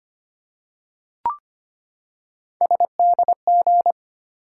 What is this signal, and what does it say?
Morse code, heard as a keyed tone, about 25 wpm, 700 Hz. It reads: SDG
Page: morse-code-ninja-practice-25wpm